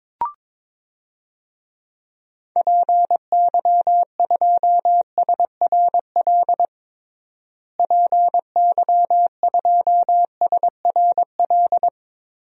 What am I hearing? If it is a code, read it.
Morse code, heard as a keyed tone, about 22 wpm, 700 Hz. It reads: PY2SRL PY2SRL
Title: DXCC-100 Call Signs - 6 Characters - Repeated 2x - 22wpm